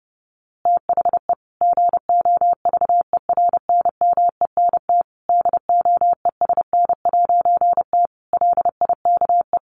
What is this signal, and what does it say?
Morse code, heard as a keyed tone, about 30 wpm, 700 Hz. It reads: THE GOVERNMENT DOESN'T LIKE
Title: A Sound Of Thunder 30wpm